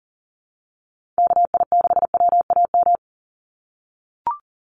Morse code, heard as a keyed tone, about 40 wpm, 700 Hz. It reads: KI6WAM
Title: USA Call Signs 40wpm